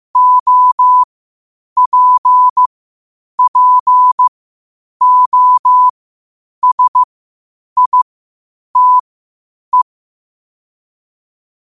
Morse code, heard as a keyed tone, about 15 wpm, spaced as for 8 wpm, 1000 Hz. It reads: OPPOSITE
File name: qotdfiles / 8wpm